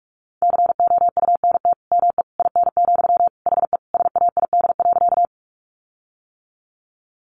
Morse code, heard as a keyed tone, about 44 wpm, 700 Hz. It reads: COUNT ME IN, HE SAID.
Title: Sentences from Top 1500 Words 45wpm